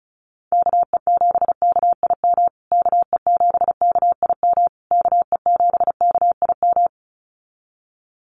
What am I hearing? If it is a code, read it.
Morse code, heard as a keyed tone, about 34 wpm, 700 Hz. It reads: KE7KIM KE7KIM KE7KIM
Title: USA Call Signs Repeated 3x 35wpm